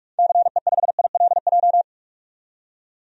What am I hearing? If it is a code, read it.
Morse code, heard as a keyed tone, about 46 wpm, 700 Hz. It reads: CE5ILJ
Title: DXCC-100 Call Signs - 6 Characters - 45wpm